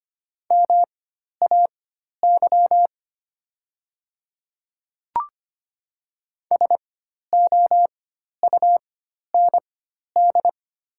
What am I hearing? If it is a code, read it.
Morse code, heard as a keyed tone, about 25 wpm, spaced as for 12 wpm, 700 Hz. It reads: MAY SOUND
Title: 4x Character Spacing - Top 100 Words - 25wpm